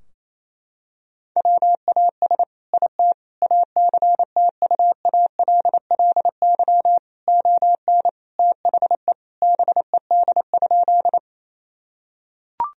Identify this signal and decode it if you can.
Morse code, heard as a keyed tone, about 28 wpm, 700 Hz. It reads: WAS IT ACTUALLY ON THE BED?
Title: Sentences from Top 1000 Words - Review Part 4 - 28wpm